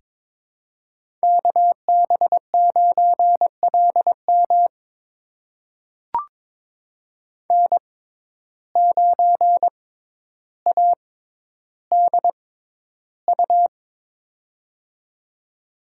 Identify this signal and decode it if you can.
Morse code, heard as a keyed tone, about 22 wpm, spaced as for 11 wpm, 700 Hz. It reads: KB9LM N9ADU
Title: USA Call Signs - 5 Characters - 6x Character Spacing - 22wpm